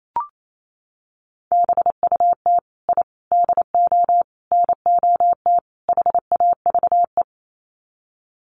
Morse code, heard as a keyed tone, about 28 wpm, 700 Hz. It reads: BUT I DO NOT HAVE
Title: Sets of 5 Words Encore 28wpm